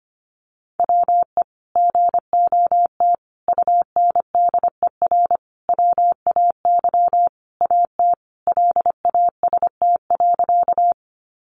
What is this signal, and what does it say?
Morse code, heard as a keyed tone, about 25 wpm, 700 Hz. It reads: WE GOT UNDER WAY AT LAST.
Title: Sentences from Top 300 Words 25wpm